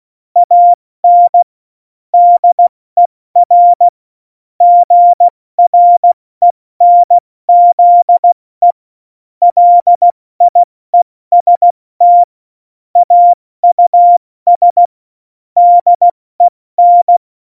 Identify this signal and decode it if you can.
Morse code, heard as a keyed tone, about 16 wpm, 700 Hz. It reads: AN DER GRENZE LIEST AUS DEN
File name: Morsekurs / Witze / Witz 04. Dezember, 70 F80